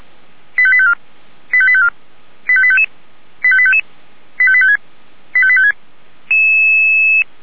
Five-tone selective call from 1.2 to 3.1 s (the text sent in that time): 85854 85855